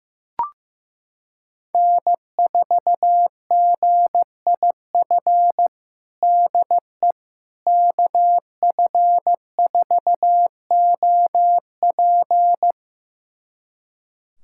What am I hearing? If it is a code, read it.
Morse code, heard as a keyed tone, about 15 wpm, 700 Hz. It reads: N4GIF DE KF4OP
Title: USA Callsign DE Callsign - 5 Characters - 15wpm